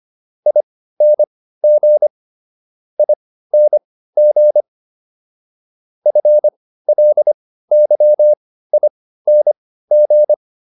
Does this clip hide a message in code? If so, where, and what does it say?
Yes: Morse code from 0.5 to 10.3 s, heard as a keyed tone, about 25 wpm, spaced as for 15 wpm, 600 Hz. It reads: ING ING FLYING